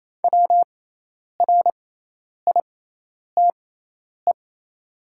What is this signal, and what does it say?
Morse code, heard as a keyed tone, about 28 wpm, spaced as for 10 wpm, 700 Hz. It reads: WRITE
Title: 6x Character Spacing - Top 100 Words - 28wpm